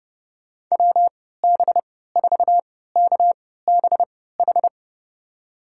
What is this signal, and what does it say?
Morse code, heard as a keyed tone, about 30 wpm, spaced as for 17 wpm, 700 Hz. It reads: WB4KBH
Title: USA Call Signs - 6 Characters - 3x Character Spacing - 30wpm